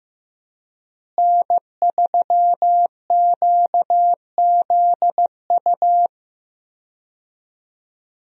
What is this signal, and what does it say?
Morse code, heard as a keyed tone, about 15 wpm, 700 Hz. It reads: N3QZU
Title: USA Call Signs - 5 Characters - Repeated 3x - 15wpm